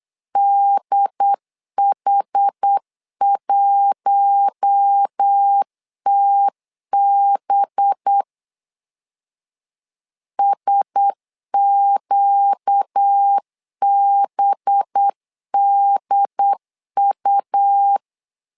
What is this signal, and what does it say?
Morse code, heard as a keyed tone, about 8 wpm, 800 Hz. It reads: DH1TB SQBDU
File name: LEZIONI / 5 LBVC / 5 LB 8wpm 2816